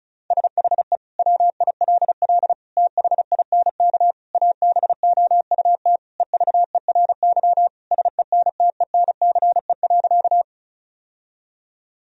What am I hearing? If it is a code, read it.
Morse code, heard as a keyed tone, about 36 wpm, 700 Hz. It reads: SHE WILL THINK ABOUT EVERY SENTENCE.
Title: Sentences from Top 1000 Words - Review Part 1 - 35wpm